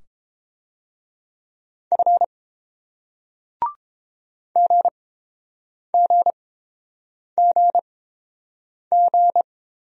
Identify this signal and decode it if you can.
Morse code, heard as a keyed tone, about 30 wpm, spaced as for 7 wpm, 700 Hz. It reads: F GGGG